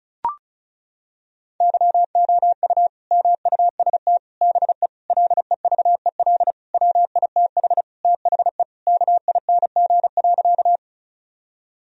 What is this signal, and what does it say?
Morse code, heard as a keyed tone, about 34 wpm, 700 Hz. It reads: YOU MUST BE LEVEL WITH THE KING.
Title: Sentences from Top 1000 Words - Review Part 4 - 35wpm